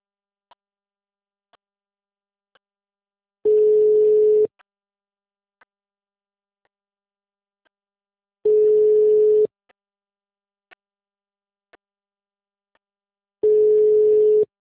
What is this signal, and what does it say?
Ringback tone (Europe, 425 Hz).